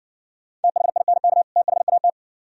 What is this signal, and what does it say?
Morse code, heard as a keyed tone, about 58 wpm, 700 Hz. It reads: THINK THAT